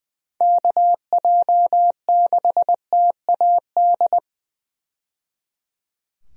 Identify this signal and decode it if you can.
Morse code, heard as a keyed tone, about 20 wpm, 700 Hz. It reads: KJ6TAD
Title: USA Call Signs - 6 Characters - 20wpm